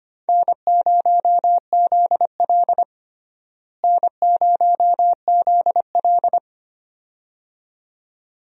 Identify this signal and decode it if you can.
Morse code, heard as a keyed tone, about 25 wpm, 700 Hz. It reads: N0ZL N0ZL
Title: USA Call Signs - 4 Characters - Repeated 2x - 25wpm